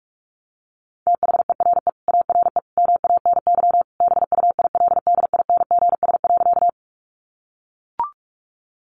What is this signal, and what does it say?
Morse code, heard as a keyed tone, about 44 wpm, 700 Hz. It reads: THERE ARE MANY BUILDINGS.
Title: Easy Phrases 45wpm